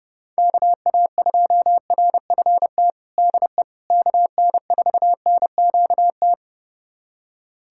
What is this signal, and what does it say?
Morse code, heard as a keyed tone, about 30 wpm, 700 Hz. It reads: KA2RFT DE KN4NQT
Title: USA Callsign DE Callsign 30wpm